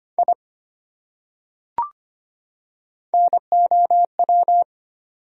Morse code, heard as a keyed tone, about 25 wpm, 700 Hz. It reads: I NOW
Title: Top 100 Words 25wpm